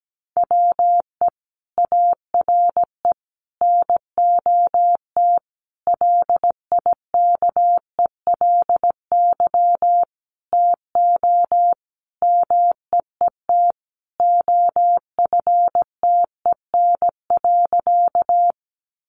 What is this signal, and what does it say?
Morse code, heard as a keyed tone, about 17 wpm, 700 Hz. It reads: WE ARE NOT LIKELY TO MEET OFTEN.
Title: Sentences from Top 1100 Words 17wpm